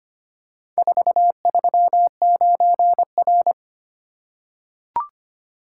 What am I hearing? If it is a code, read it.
Morse code, heard as a keyed tone, about 25 wpm, 700 Hz. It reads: 439R
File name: practice / 25wpm / Four Letter-Number Combinations - 25wpm